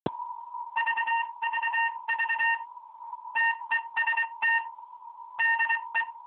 Morse code, heard as a keyed tone, about 24 wpm, spaced as for 21 wpm, 1800 Hz, with noise. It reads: 5VV TEST DE